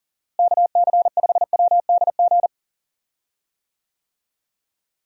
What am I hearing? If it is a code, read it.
Morse code, heard as a keyed tone, about 40 wpm, 700 Hz. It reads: KC5WDG
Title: USA Call Signs - 6 Characters - Repeated 3x - 40wpm